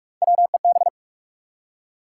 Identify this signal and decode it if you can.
Morse code, heard as a keyed tone, about 44 wpm, 700 Hz. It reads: WEB
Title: Three Letter Words - 45wpm